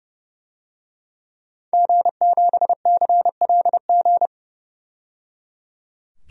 Morse code, heard as a keyed tone, about 29 wpm, 700 Hz. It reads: G7CLG
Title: DXCC-100 Call Signs - 5 Characters - 30wpm